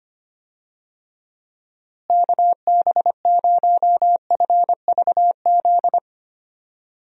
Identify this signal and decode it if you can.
Morse code, heard as a keyed tone, about 25 wpm, 700 Hz. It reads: KB0FVZ
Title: USA Call Signs - 6 Characters - Repeated 3x - 25wpm